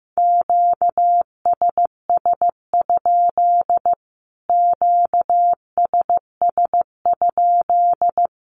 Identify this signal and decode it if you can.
Morse code, heard as a keyed tone, about 15 wpm, 700 Hz. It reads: QSS? QSS?
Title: Q-Codes Repeated 3x 15wpm